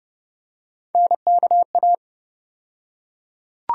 Morse code, heard as a keyed tone, about 32 wpm, 700 Hz. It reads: NKA
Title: USA Call Sign Postfixes - 3 Letters 30wpm